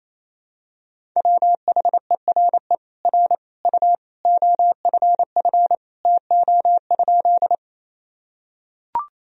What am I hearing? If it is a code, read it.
Morse code, heard as a keyed tone, about 28 wpm, 700 Hz. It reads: WHERE R U OFF TO?
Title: Sentences from Top 200 Words and Abbreviations 28wpm